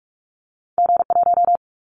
Morse code, heard as a keyed tone, about 46 wpm, 700 Hz. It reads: G1
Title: Callsign Prefixes for Top 25 DXCC - 45wpm